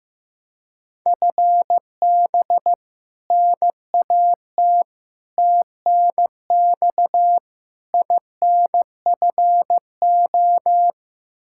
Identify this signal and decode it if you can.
Morse code, heard as a keyed tone, about 15 wpm, 700 Hz. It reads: FB NAT TNX INFO
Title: Practice QSO - Exchange 3 - More Info - 15wpm